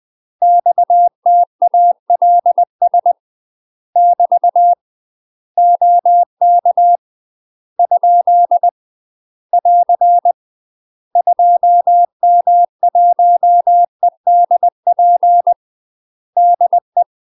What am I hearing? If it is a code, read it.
Morse code, heard as a keyed tone, about 20 wpm, 700 Hz. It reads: XTALS <BT> OK ? <AR> 2M1EDP DE
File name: UK MASTER QSOs TRAINING / 20WPM / UK MASTER QSOs 0010